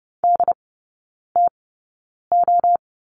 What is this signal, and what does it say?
Morse code, heard as a keyed tone, about 30 wpm, spaced as for 9 wpm, 700 Hz. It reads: DTO
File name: 081 Lesson 14e - Number 2 - Three Characters 30wpm